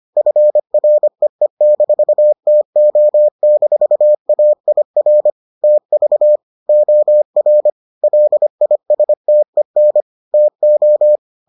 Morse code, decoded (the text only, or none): FREE-TO-AIR TV OR LISTEN TO